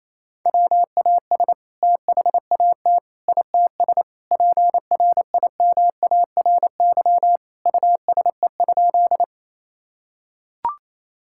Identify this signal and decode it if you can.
Morse code, heard as a keyed tone, about 28 wpm, 700 Hz. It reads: WAS THAT ITS PRIMARY USE?
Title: Sentences from Top 1000 Words - Review Part 5 - 28wpm